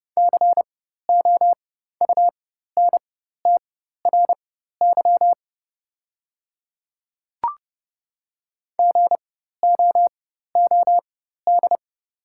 Morse code, heard as a keyed tone, about 30 wpm, spaced as for 14 wpm, 700 Hz. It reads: COUNTRY GOOD
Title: 4x Character Spacing - Top 300 Words - 30wpm